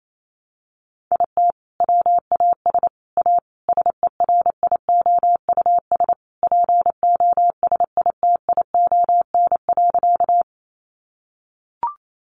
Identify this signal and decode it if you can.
Morse code, heard as a keyed tone, about 28 wpm, 700 Hz. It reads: IT WAS A SERIOUS POSITION.
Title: Sentences from Top 1100 Words 28wpm